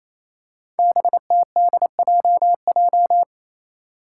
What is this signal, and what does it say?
Morse code, heard as a keyed tone, about 28 wpm, 700 Hz. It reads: BTDJJ
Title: Five Challenging Letter-Number Combinations 28wpm